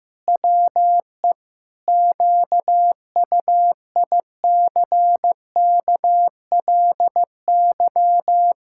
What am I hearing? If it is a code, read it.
Morse code, heard as a keyed tone, about 15 wpm, 700 Hz. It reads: WE QUICKLY
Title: Pangram Challenge v2 - 15wpm